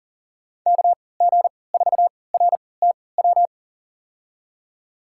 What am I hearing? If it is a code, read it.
Morse code, heard as a keyed tone, about 40 wpm, spaced as for 23 wpm, 700 Hz. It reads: KG4RTW
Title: USA Call Signs - 6 Characters - 3x Character Spacing - 40wpm